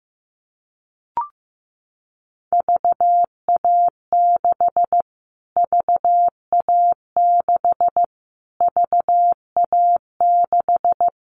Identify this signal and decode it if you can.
Morse code, heard as a keyed tone, about 15 wpm, 700 Hz. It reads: VA6 VA6 VA6
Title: Callsign Prefixes for Top 25 DXCC - Repeated 3x - 15wpm